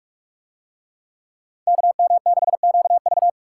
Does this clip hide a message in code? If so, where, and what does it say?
Yes: Morse code from 1.7 to 3.3 s, heard as a keyed tone, about 43 wpm, 700 Hz. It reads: KM6QV